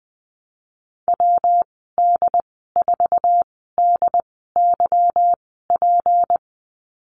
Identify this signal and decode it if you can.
Morse code, heard as a keyed tone, about 20 wpm, spaced as for 15 wpm, 700 Hz. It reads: WD4DYP
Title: 193 Lesson 31g - Letter v - Call Signs 20wpm